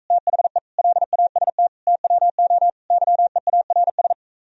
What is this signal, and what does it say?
Morse code, heard as a keyed tone, about 45 wpm, 700 Hz. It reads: THE LAST TWO YEARS